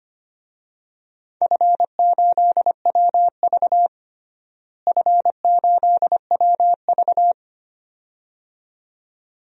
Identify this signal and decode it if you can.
Morse code, heard as a keyed tone, about 25 wpm, 700 Hz. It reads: F8WV F8WV
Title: DXCC-100 Call Signs - 4 Characters - Repeated 2x - 25wpm